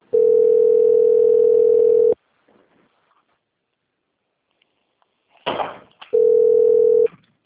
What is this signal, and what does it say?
Ringback tone (North America), 440 + 480 Hz.
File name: out- - - - - .53857